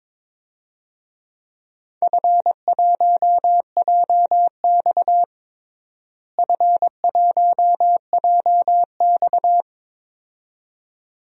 Morse code, heard as a keyed tone, about 22 wpm, 700 Hz. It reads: F1JX F1JX
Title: DXCC-100 Call Signs - 4 Characters - Repeated 2x - 22wpm